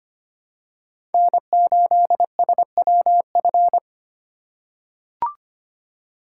Morse code, heard as a keyed tone, about 25 wpm, 700 Hz. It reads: N8SWF
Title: USA Call Signs - 4x Character Spacing - 25wpm